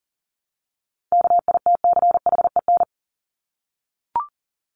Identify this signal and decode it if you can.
Morse code, heard as a keyed tone, about 40 wpm, 700 Hz. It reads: KITCHEN
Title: Top 1251-1500 Words 40wpm